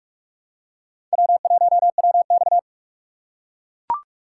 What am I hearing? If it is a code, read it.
Morse code, heard as a keyed tone, about 44 wpm, 690 Hz. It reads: W1WX
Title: USA Call Signs - 4 Characters - 3x Character Spacing - 45wpm